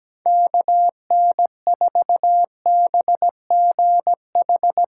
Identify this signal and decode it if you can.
Morse code, heard as a keyed tone, about 17 wpm, 700 Hz. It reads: KN4BGH
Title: USA Call Signs - 6 Characters - 6x Character Spacing - 17wpm